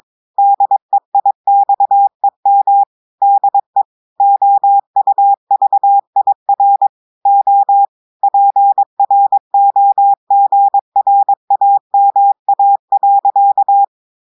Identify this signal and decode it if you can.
Morse code, heard as a keyed tone, about 22 wpm, 800 Hz. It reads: DEIXEM DE OUVIR O PROGRAMA.
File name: audios / portugais / portugais texte en clair 2 vitesse 22 mots minutes